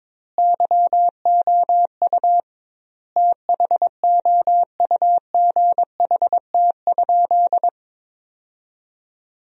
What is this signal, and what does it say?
Morse code, heard as a keyed tone, about 22 wpm, 700 Hz. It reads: YOU THOUGHT?
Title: 2x Word-Spacing - Sentences from Top 300 Words - 22wpm